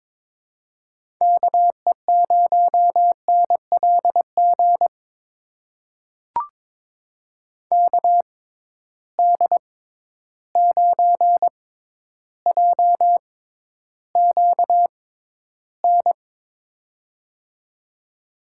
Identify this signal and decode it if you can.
Morse code, heard as a keyed tone, about 21 wpm, spaced as for 11 wpm, 700 Hz. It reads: KE0NLG KD9JQN
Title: USA Call Signs - 6 Characters - 6x Character Spacing - 22wpm